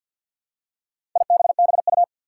Morse code, heard as a keyed tone, about 53 wpm, 700 Hz. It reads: IBBU
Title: Four Challenging Letter-Number Combinations 50wpm